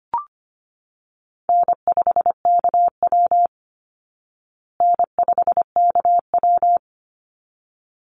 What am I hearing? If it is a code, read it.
Morse code, heard as a keyed tone, about 25 wpm, 700 Hz. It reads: N5KW N5KW
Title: Easy Call Signs Repeated 2x - 25wpm